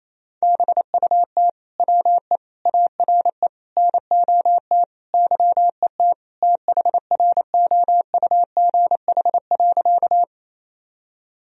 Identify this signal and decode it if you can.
Morse code, heard as a keyed tone, about 28 wpm, 700 Hz. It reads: BUT WE ARE NOT YET THROUGH.
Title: Sentences from Top 1000 Words - Review Part 1 - 28wpm